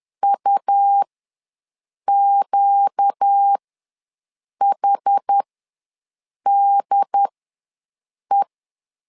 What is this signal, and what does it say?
Morse code, heard as a keyed tone, about 10 wpm, spaced as for 6 wpm, 800 Hz. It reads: UQHDE
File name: LEZIONI / 4 Q ws 10wpm 2306